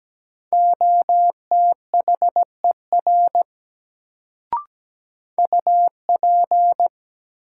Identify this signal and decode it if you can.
Morse code, heard as a keyed tone, about 17 wpm, 700 Hz. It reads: OTHER UP
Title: Sentences from Top 100 Words - New Word Drill - 17wpm